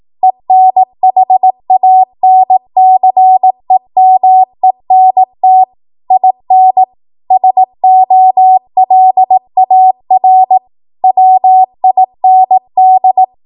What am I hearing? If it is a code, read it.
Morse code, heard as a keyed tone, about 18 wpm, 750 Hz. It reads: ENHANCEMENT IN SOLAR WIND
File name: Morse / 18WPM